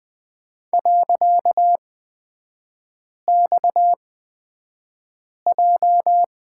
Morse code, heard as a keyed tone, about 20 wpm, spaced as for 5 wpm, 700 Hz. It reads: .XJ
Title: 255 Lesson 41e - Period - Three Characters 20wpm